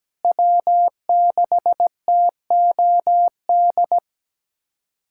Morse code, heard as a keyed tone, about 17 wpm, 700 Hz. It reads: W6TOD
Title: USA Callsign DE Callsign - 5 Characters - 17wpm